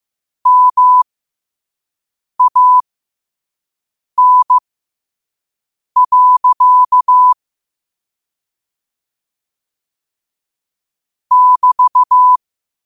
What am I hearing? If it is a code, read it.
Morse code, heard as a keyed tone, about 15 wpm, spaced as for 5 wpm, 1000 Hz. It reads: MAN. <BT>
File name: qotdfiles / 5wpm